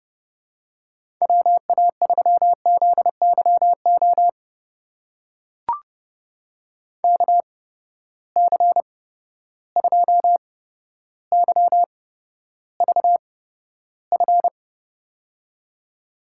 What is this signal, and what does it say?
Morse code, heard as a keyed tone, about 30 wpm, spaced as for 13 wpm, 700 Hz. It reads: WA3ZYO KC2YVF
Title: USA Call Signs - 6 Characters - 8x Character Spacing - 30wpm